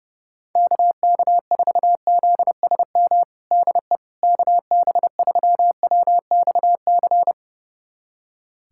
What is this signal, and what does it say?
Morse code, heard as a keyed tone, about 30 wpm, 700 Hz. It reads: KK4ZSM DE KB3WXC
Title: USA Callsign DE Callsign - 6 Characters - 30wpm